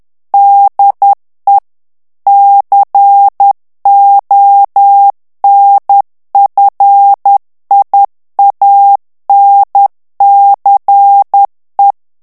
Morse code, heard as a keyed tone, about 10 wpm, 800 Hz. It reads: DE CONFIANCE
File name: morse mp3 / FRANCAIS TEXTE CLAIR 2 VITESSE 10 MOTS MINUTES